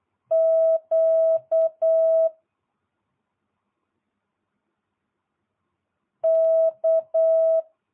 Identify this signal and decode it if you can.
Morse code, heard as a keyed tone, about 8 wpm, 650 Hz. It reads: Q K